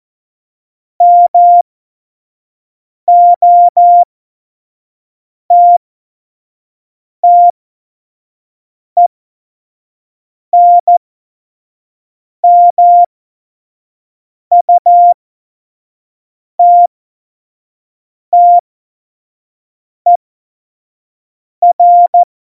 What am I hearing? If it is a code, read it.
Morse code, heard as a keyed tone, about 14 wpm, spaced as for 5 wpm, 700 Hz. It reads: MOTTENMUTTER
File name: Morsekurs / Witze / Witz 05. Juni 2018, 25 F70